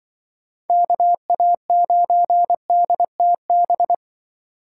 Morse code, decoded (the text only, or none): KA9DTB